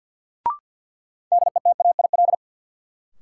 Morse code, heard as a keyed tone, about 48 wpm, 700 Hz. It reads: DETAIL
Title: Sentences from Top 1500 Words - New Word Drill - 50wpm